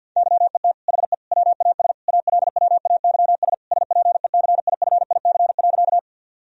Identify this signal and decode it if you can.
Morse code, heard as a keyed tone, about 51 wpm, 700 Hz. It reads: YET HE WAS ALWAYS SPECIFIC.
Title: Sentences from Top 1000 Words - Review Part 3 - 50wpm